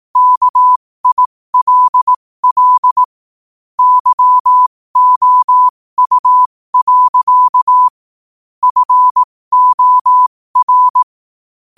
Morse code, heard as a keyed tone, about 18 wpm, spaced as for 15 wpm, 1000 Hz. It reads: KILL YOU. FOR
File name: qotdfiles / 15wpm